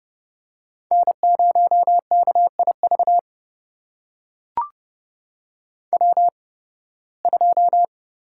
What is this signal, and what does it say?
Morse code, heard as a keyed tone, about 30 wpm, spaced as for 20 wpm, 700 Hz. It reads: N0KIV W2
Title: USA Call Signs - 5 Characters - 8x Character Spacing - 30wpm